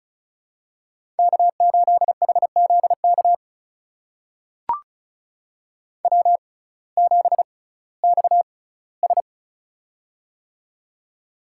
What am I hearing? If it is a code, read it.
Morse code, heard as a keyed tone, about 36 wpm, spaced as for 19 wpm, 700 Hz. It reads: K8HZK W7XS
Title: USA Call Signs - 6x Character Spacing - 35wpm